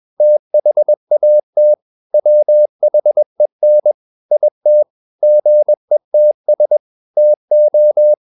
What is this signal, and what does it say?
Morse code, heard as a keyed tone, about 21 wpm, 600 Hz. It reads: THAT WHEN IT GETS TO
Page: CW med Gnister